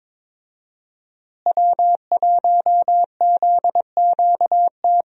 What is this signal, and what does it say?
Morse code, heard as a keyed tone, about 22 wpm, 700 Hz. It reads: W1ZQT
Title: USA Call Signs - 5 Characters - Repeated 2x - 22wpm